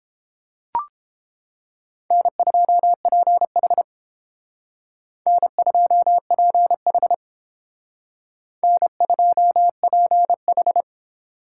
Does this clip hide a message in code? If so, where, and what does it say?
Yes: Morse code from 2.1 to 10.8 s, heard as a keyed tone, about 30 wpm, 700 Hz. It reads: N2PH N2PH N2PH